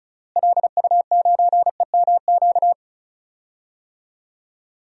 Morse code, heard as a keyed tone, about 34 wpm, 700 Hz. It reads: LU9EMQ
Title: DXCC-100 Call Signs - 6 Characters - Repeated 2x - 35wpm